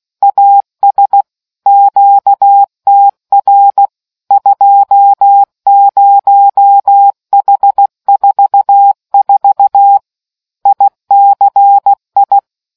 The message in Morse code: AS QTR 20H44 ICI

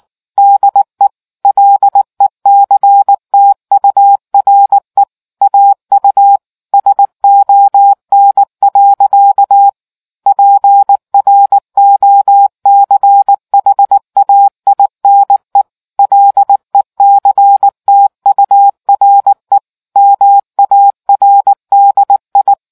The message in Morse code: DE LECTURE AU SON. PROCHAINE LECTURE MARDI

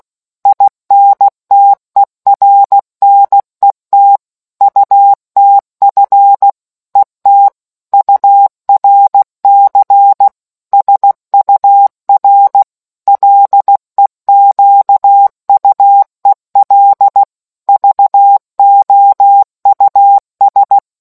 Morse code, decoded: INTERNET UTF ET URC SUR LEQUEL VOUS

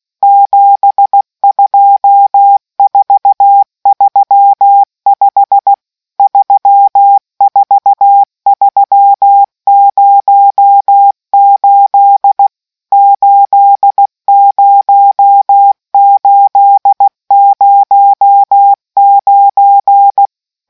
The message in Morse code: 72435 34308 80809

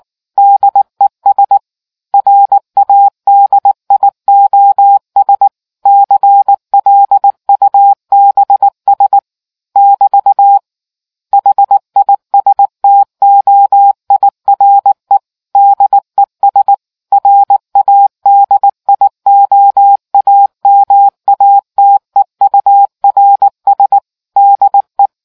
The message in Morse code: DES RADIOS CLUBS <BT> HISTOIRE DES RADIOAMATEURS DE

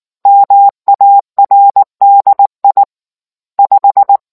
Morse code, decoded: MARDI 5

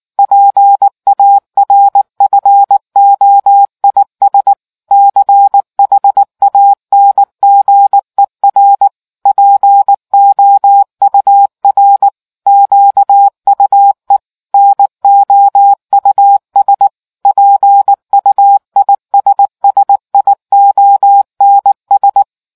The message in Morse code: PARFOIS CHANGER POUR QUE NOUS PUISSIONS

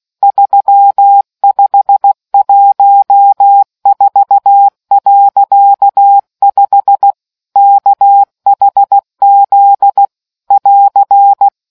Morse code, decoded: 3514.5 KHZ <AR>

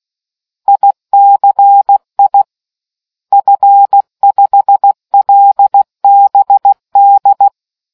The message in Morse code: ICI F5LBD

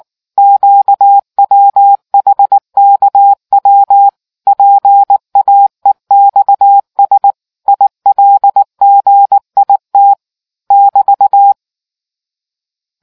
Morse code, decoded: QWHKW PAEXS ILGIT <BT>

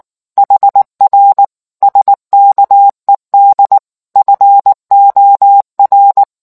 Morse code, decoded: HR SKED FOR

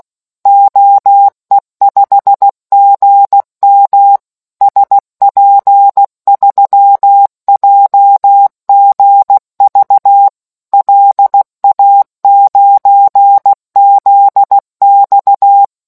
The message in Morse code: OE5GM SP3JGV LA9ZX

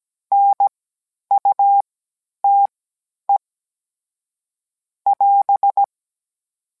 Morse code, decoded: NUTE <AS>

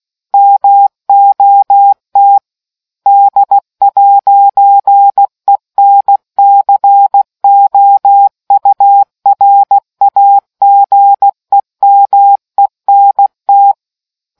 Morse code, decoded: MOT D'ENCOURAGEMENT

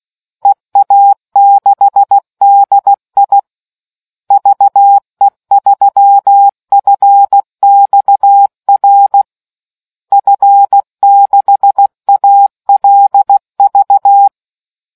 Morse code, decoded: EA6DI VE3FXR F6ALV

